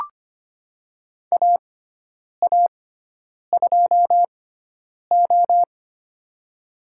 Morse code, heard as a keyed tone, about 25 wpm, spaced as for 9 wpm, 700 Hz. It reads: AA2O
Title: USA Call Signs - 4 Characters - 6x Character Spacing - 25wpm